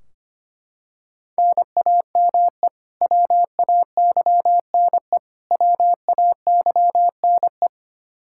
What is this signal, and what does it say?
Morse code, heard as a keyed tone, about 25 wpm, 700 Hz. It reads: NAME WAYNE WAYNE
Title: Practice QSO - Exchange 1 - Beginning - 25wpm